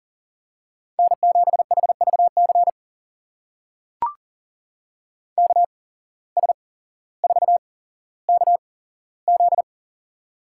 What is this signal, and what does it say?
Morse code, heard as a keyed tone, about 40 wpm, spaced as for 17 wpm, 700 Hz. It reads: N7HVC KS4KZ
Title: USA Call Signs - 5 Characters - 8x Character Spacing - 40wpm